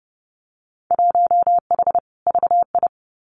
Morse code, heard as a keyed tone, about 32 wpm, 700 Hz. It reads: 1H VI
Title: Winter Field Day Exchanges Encore 30wpm